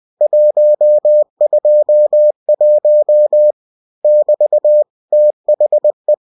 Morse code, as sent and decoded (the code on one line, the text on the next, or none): .---- ..--- .---- / -...- / - .... .
121 <BT> THE